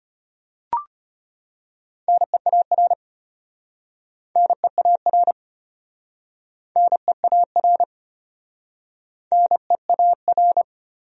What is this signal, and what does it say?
Morse code, heard as a keyed tone, about 33 wpm, 700 Hz. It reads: NEAR NEAR NEAR NEAR